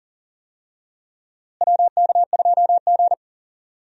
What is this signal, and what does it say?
Morse code, heard as a keyed tone, about 40 wpm, 700 Hz. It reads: WK2G
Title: USA Call Signs - 4 Characters - Repeated 3x - 40wpm